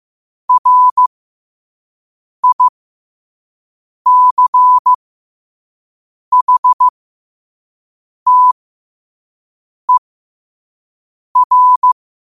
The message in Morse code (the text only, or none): RICHTER